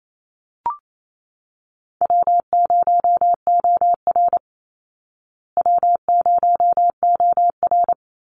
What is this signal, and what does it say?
Morse code, heard as a keyed tone, about 28 wpm, 700 Hz. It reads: W0OR W0OR
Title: Easy Call Signs Repeated 3x - 28wpm